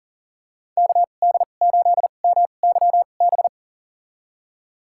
Morse code, heard as a keyed tone, about 38 wpm, spaced as for 28 wpm, 700 Hz. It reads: KD8MYB
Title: USA Call Signs - 6 Characters - 2x Character Spacing - 40wpm